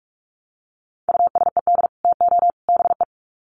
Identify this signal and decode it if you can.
Morse code, heard as a keyed tone, about 44 wpm, 700 Hz. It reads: USED TO BE
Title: Sets of 3 Words 45wpm